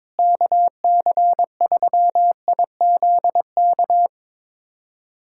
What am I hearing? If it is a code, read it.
Morse code, heard as a keyed tone, about 22 wpm, 700 Hz. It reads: KC3IZK
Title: USA Callsign DE Callsign 22wpm